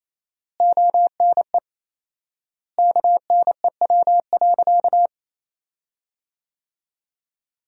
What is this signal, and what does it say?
Morse code, heard as a keyed tone, about 28 wpm, 700 Hz. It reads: ONE KNEW.
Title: 4x Word-Spacing - Sentences from Top 400 Words - 28wpm